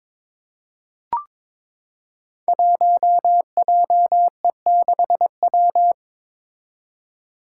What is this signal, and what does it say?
Morse code, heard as a keyed tone, about 21 wpm, 700 Hz. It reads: 1JE6W
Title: Five Challenging Letter-Number Combinations 22wpm